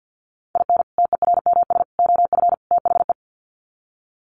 Morse code, heard as a keyed tone, about 50 wpm, 700 Hz. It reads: IN TERMS OF THE
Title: Sets of 4 Words 50wpm